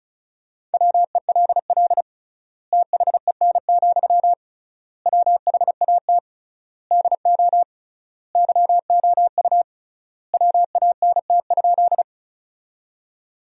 Morse code, heard as a keyed tone, about 36 wpm, 700 Hz. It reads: WELL THEN, WHAT DO YOU WANT?
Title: 3x Word-Spacing - Sentences from Top 200 Words - 35wpm